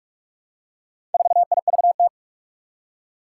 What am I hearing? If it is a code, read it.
Morse code, heard as a keyed tone, about 45 wpm, 700 Hz. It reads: 4IVT
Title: Four Challenging Letter-Number Combinations 45wpm